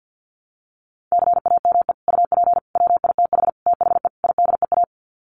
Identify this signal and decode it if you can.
Morse code, heard as a keyed tone, about 50 wpm, 700 Hz. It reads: CAME UP WITH THE IDEA
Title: Sets of 5 Words Encore2 50wpm